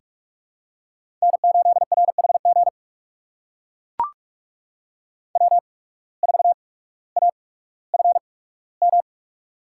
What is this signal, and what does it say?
Morse code, heard as a keyed tone, about 44 wpm, spaced as for 19 wpm, 700 Hz. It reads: N8RHG W4AFM
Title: USA Call Signs - 5 Characters - 8x Character Spacing - 45wpm